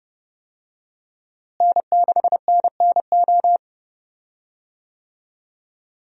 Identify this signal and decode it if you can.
Morse code, heard as a keyed tone, about 29 wpm, 700 Hz. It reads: N6NNO
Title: USA Call Signs - 5 Characters - Repeated 3x - 30wpm